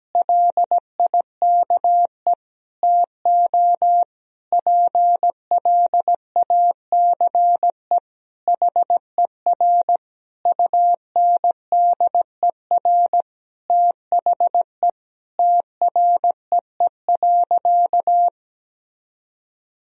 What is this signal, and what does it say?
Morse code, heard as a keyed tone, about 17 wpm, 700 Hz. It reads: LIKE TO PLACE HER UNDER THE TREE.
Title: Sentences from Top 1000 Words - Review Part 1 - 17wpm